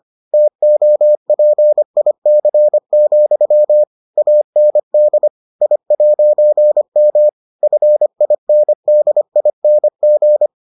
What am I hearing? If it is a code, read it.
Morse code, heard as a keyed tone, about 25 wpm, 600 Hz. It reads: TOPIC, AND I'M FINDING